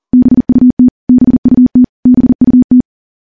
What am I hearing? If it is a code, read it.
Morse code, heard as a keyed tone, about 39 wpm, 250 Hz. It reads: BUT BUT BUT
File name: morse / morse-training-40WPM